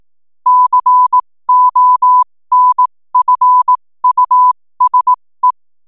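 Morse code, heard as a keyed tone, about 18 wpm, spaced as for 15 wpm, 1000 Hz. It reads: CONFUSE
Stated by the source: 15 wpm Morse code from RNZN Communicators Association.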